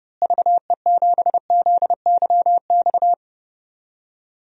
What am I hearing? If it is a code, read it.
Morse code, heard as a keyed tone, about 30 wpm, 700 Hz. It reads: VE7ZYX
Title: DXCC-100 Call Signs - 6 Characters - Repeated 2x - 30wpm